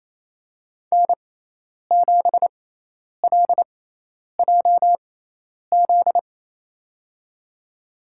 Morse code, heard as a keyed tone, about 28 wpm, spaced as for 10 wpm, 700 Hz. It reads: N7LJZ
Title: USA Call Signs - 6x Character Spacing - 28wpm